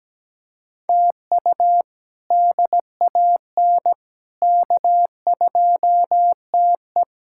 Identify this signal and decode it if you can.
Morse code, heard as a keyed tone, about 17 wpm, 700 Hz. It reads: TU DAN K2TE
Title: K1USN SST QSO Practice - 17wpm